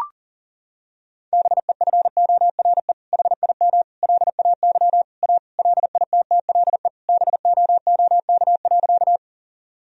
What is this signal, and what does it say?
Morse code, heard as a keyed tone, about 40 wpm, 700 Hz. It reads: BEFORE HIM LAY A LITTLE BOOK.
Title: Sentences from Top 1000 Words - Review Part 2 - 40wpm